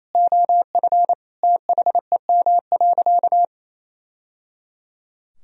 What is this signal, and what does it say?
Morse code, heard as a keyed tone, about 28 wpm, 700 Hz. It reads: OF THEM.
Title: Sentences from Top 1000 Words - Review Part 3 - 28wpm